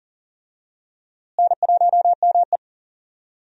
Morse code, heard as a keyed tone, about 40 wpm, 700 Hz. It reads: N1ME
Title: USA Call Signs - 4 Characters - Repeated 3x - 40wpm